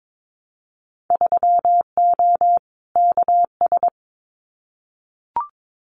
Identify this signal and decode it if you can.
Morse code, heard as a keyed tone, about 22 wpm, 700 Hz. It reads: 3O KS
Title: Winter Field Day Exchanges 22wpm